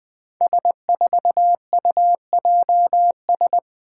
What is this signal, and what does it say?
Morse code, heard as a keyed tone, about 20 wpm, 700 Hz. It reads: S4UJS
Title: Five Challenging Letter-Number Combinations 20wpm